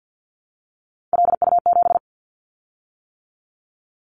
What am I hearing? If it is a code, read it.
Morse code, heard as a keyed tone, about 50 wpm, 700 Hz. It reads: LU7
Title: Callsign Prefixes for Top 50 DXCC - 50wpm